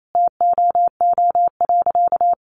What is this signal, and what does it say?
Morse code, heard as a keyed tone, about 28 wpm, 700 Hz. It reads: TOO.
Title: Sentences from Top 1400 Words 28wpm